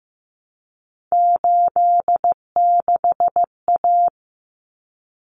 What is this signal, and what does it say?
Morse code, heard as a keyed tone, about 15 wpm, 700 Hz. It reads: 86A
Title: Common Cut Numbers - 3-Digit 15wpm